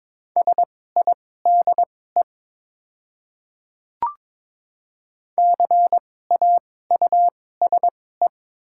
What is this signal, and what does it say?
Morse code, heard as a keyed tone, about 22 wpm, spaced as for 16 wpm, 700 Hz. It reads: SIDE CAUSE
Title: 2x Character Spacing - Top 200 Words - 22wpm